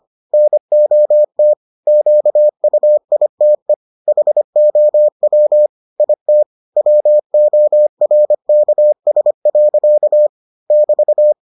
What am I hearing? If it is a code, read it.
Morse code, heard as a keyed tone, about 25 wpm, 600 Hz. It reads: NOT QUITE HOW IT WORKS. <BT>